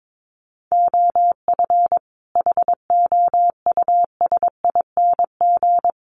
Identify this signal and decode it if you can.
Morse code, heard as a keyed tone, about 22 wpm, 700 Hz. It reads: OF HOUSING
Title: Sets of 5 Words Encore2 22wpm